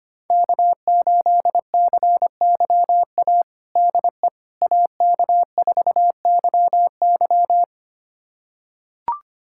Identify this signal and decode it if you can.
Morse code, heard as a keyed tone, about 25 wpm, 700 Hz. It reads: K8CYA DE AK4YY
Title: USA Callsign DE Callsign - 5 Characters - 25wpm